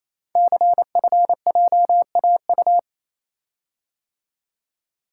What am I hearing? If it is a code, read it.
Morse code, heard as a keyed tone, about 29 wpm, 700 Hz. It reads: CFJAU
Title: Code Groups - 5 Letters - 28wpm